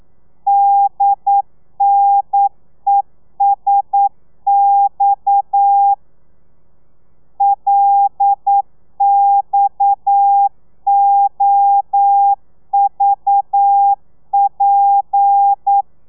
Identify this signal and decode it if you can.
Morse code, heard as a keyed tone, about 9 wpm, 800 Hz. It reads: DNESX LXOVP
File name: telegrafia / i2tfj / Lez7 40